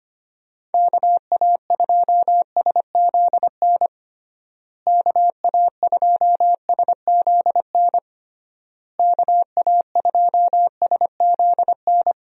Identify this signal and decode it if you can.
Morse code, heard as a keyed tone, about 25 wpm, 700 Hz. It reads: KA2SZN KA2SZN KA2SZN
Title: USA Call Signs - 6 Characters - Repeated 3x - 25wpm